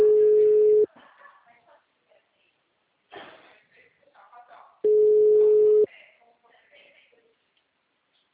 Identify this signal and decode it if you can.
Ringback tone (Europe, 425 Hz), with speech.